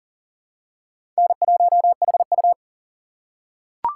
Morse code, heard as a keyed tone, about 40 wpm, 700 Hz. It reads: N1HU
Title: USA Call Signs - 6x Character Spacing - 40wpm